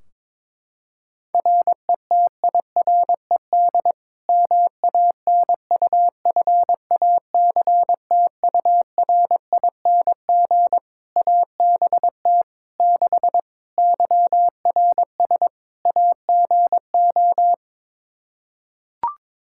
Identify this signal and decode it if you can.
Morse code, heard as a keyed tone, about 22 wpm, 700 Hz. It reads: RETIRED MANUFACTURING ABT 6 YRS AGO
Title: Practice QSO - Exchange 3 - More Info - 22wpm